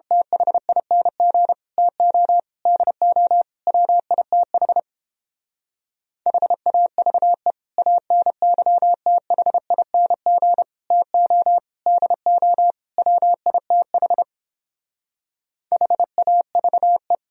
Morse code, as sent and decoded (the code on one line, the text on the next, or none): - .... .. -. --. / - --- / -.. --- / .-- .. - .... / .... .- ...- . / .- -. -.-- - .... .. -. --. / - --- / -.. --- / .-- .. - .... / .... .- ...- .
THING TO DO WITH HAVE ANYTHING TO DO WITH HAVE